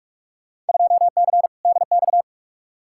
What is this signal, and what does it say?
Morse code, heard as a keyed tone, about 46 wpm, 700 Hz. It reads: 2C DX
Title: Field Day Exchanges Encore - 45wpm